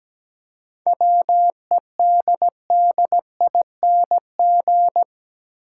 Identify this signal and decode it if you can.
Morse code, heard as a keyed tone, about 17 wpm, 700 Hz. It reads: WEDDING
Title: Seven Letter Words - 17wpm